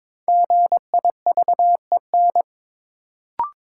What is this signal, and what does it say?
Morse code, heard as a keyed tone, about 22 wpm, 700 Hz. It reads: GIVEN
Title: Sentences from Top 1400 Words - New Word Drill - 22wpm